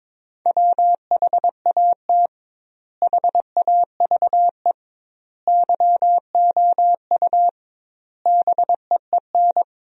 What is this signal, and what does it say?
Morse code, heard as a keyed tone, about 22 wpm, 700 Hz. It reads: WHAT HAVE YOU BEEN
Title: 2x Word-Spacing - Sentences from Top 200 Words - 22wpm